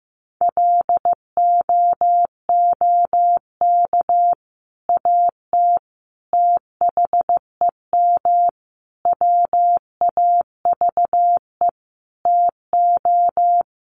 Morse code, heard as a keyed tone, about 15 wpm, 700 Hz. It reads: LOOK AT THEM WAVE TO
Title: Sentences from Top 1400 Words 15wpm